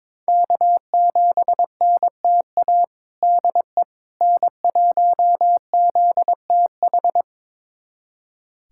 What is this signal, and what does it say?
Morse code, heard as a keyed tone, about 22 wpm, 700 Hz. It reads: K7NTA DE N1ZTH
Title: USA Callsign DE Callsign - 5 Characters - 22wpm